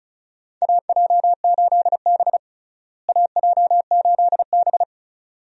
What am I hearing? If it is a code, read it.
Morse code, heard as a keyed tone, about 36 wpm, 700 Hz. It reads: AJ8B AJ8B
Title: USA Call Signs - 4 Characters - Repeated 2x - 35wpm